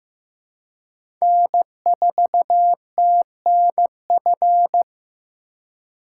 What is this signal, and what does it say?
Morse code, heard as a keyed tone, about 15 wpm, 700 Hz. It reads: N4TNF
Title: USA Call Signs - 4x Character Spacing - 15wpm